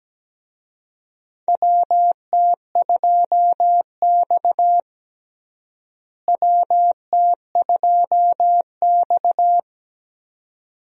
Morse code, heard as a keyed tone, about 17 wpm, 700 Hz. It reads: WT2X WT2X
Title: USA Call Signs - 4 Characters - Repeated 2x - 17wpm